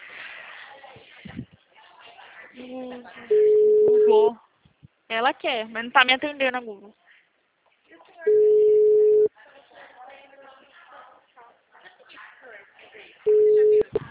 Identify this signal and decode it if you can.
Ringback tone (Europe, 425 Hz), with speech.